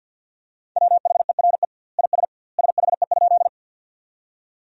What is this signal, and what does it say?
Morse code, heard as a keyed tone, about 50 wpm, 700 Hz. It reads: WHERE IS SHE?
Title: 2x Word-Spacing - Sentences from Top 200 Words - 50wpm